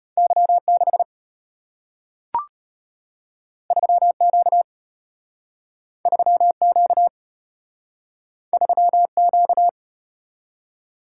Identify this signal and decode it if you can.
Morse code, heard as a keyed tone, about 33 wpm, 700 Hz. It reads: Y6 3Q 3Q 3Q